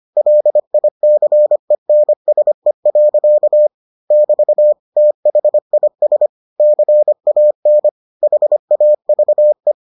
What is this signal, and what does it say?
Morse code, heard as a keyed tone, about 25 wpm, 600 Hz. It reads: LICENSE. <BT> THIS CAN HAVE